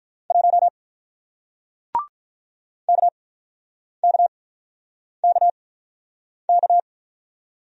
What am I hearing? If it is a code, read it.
Morse code, heard as a keyed tone, about 46 wpm, spaced as for 9 wpm, 700 Hz. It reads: 1 KKKK